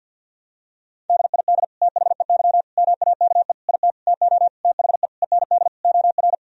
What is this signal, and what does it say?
Morse code, heard as a keyed tone, about 52 wpm, 700 Hz. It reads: DID THEY MAKE IT TO THE END OR
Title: Sentences from Top 1000 Words - Review Part 3 - 50wpm